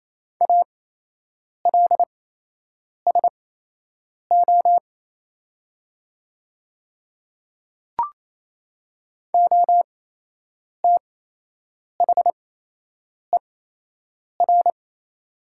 Morse code, heard as a keyed tone, about 27 wpm, spaced as for 8 wpm, 700 Hz. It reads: ALSO OTHER
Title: 8x Character Spacing - Top 300 Words - 28wpm